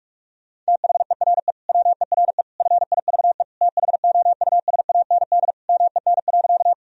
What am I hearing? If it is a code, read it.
Morse code, heard as a keyed tone, about 46 wpm, 700 Hz. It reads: THERE WERE FIVE THOUSAND MEN.
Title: Sentences from Top 1000 Words - Review Part 3 - 45wpm